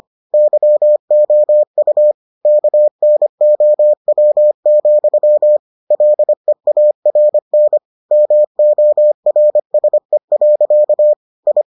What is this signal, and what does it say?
Morse code, heard as a keyed tone, about 25 wpm, 600 Hz. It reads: YOU KNOW, LEARN MORSE. I